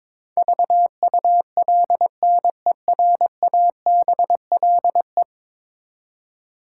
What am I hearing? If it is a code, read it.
Morse code, heard as a keyed tone, about 22 wpm, 700 Hz. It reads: VULNERABLE
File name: practice / 22wpm / 10 Letter Words - First 3 Letters Given - 22wpm